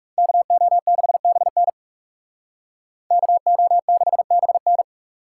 Morse code, heard as a keyed tone, about 37 wpm, 700 Hz. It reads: KO6BN KO6BN
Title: Speed Racing - USA 2x2 Call Signs 30wpm